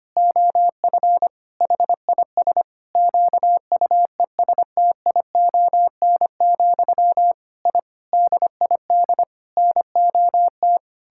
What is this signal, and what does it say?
Morse code, heard as a keyed tone, about 25 wpm, 700 Hz. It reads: OF HIS QUESTION, I DID NOT
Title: Sentences from Top 1000 Words - Review Part 6 - 25wpm